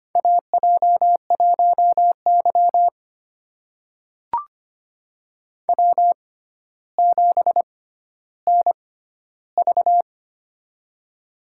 Morse code, heard as a keyed tone, about 25 wpm, spaced as for 13 wpm, 700 Hz. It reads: AJ1Y W7NV
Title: USA Call Signs - 4 Characters - 6x Character Spacing - 25wpm